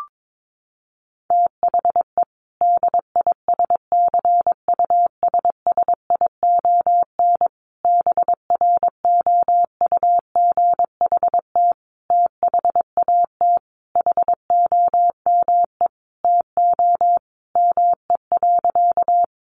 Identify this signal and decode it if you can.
Morse code, heard as a keyed tone, about 22 wpm, 700 Hz. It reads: THE DISCUSSION BROUGHT THAT HOME TO ME.
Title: Sentences from Top 1100 Words 22wpm